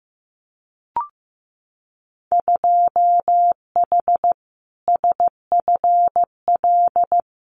Morse code, heard as a keyed tone, about 15 wpm, 700 Hz. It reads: 2H SFL
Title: Winter Field Day Exchanges Encore 15wpm